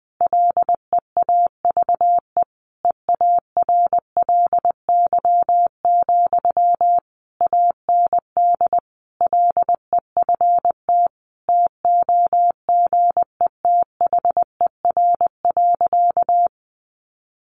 Morse code, heard as a keyed tone, about 20 wpm, 700 Hz. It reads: LEAVE EARLY, AND LEFT TOGETHER.
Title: Sentences from Top 400 Words 20wpm